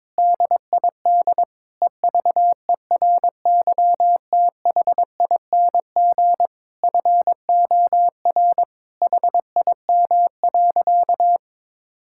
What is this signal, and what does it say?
Morse code, heard as a keyed tone, about 22 wpm, 700 Hz. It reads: DID EVERYTHING FOR HIM.
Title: Sentences from Top 1000 Words - Review Part 6 - 22wpm